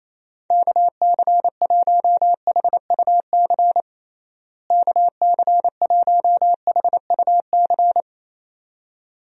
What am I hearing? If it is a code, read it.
Morse code, heard as a keyed tone, about 28 wpm, 700 Hz. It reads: KC1HUC KC1HUC
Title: USA Call Signs - 6 Characters - Repeated 2x - 28wpm